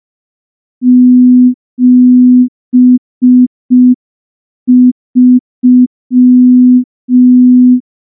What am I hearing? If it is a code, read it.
Morse code, heard as a keyed tone, about 5 wpm, 250 Hz. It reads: 73
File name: morse / morse-training-05WPM